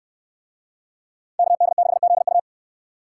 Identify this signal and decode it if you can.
Morse code, heard as a keyed tone, about 69 wpm, 700 Hz. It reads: KD6ZU